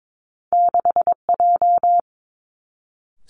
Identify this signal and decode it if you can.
Morse code, heard as a keyed tone, about 22 wpm, 700 Hz. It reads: 6J
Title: Two Challenging Letter-Number Combinations 22wpm